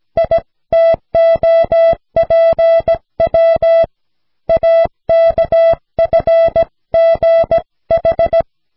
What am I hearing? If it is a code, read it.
Morse code, heard as a keyed tone, about 16 wpm, 660 Hz. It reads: ITOPW AKFGH
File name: cw / snelheid 4